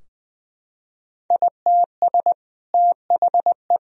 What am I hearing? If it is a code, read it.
Morse code, heard as a keyed tone, about 20 wpm, 700 Hz. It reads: ITS THE
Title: Sentences from Top 1000 Words - Review Part 3 - 20wpm